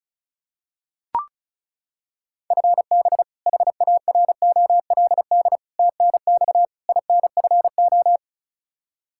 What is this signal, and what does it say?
Morse code, heard as a keyed tone, about 36 wpm, 700 Hz. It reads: FB HAROLD TNX INFO
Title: Practice QSO - Exchange 3 - More Info - 35wpm